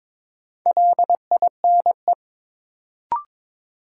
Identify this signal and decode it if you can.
Morse code, heard as a keyed tone, about 22 wpm, 700 Hz. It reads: LINE
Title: Sentences from Top 1200 Words - New Word Drill Alt - 22wpm